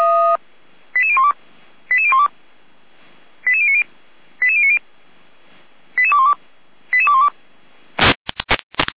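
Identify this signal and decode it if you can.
Five-tone selective call (ZVEI), heard as quick runs of five tones, tones of about 70 ms. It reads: 88012 88012 88090 88090 88212 88212
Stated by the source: ZVEI five-tone selective call received on SDR.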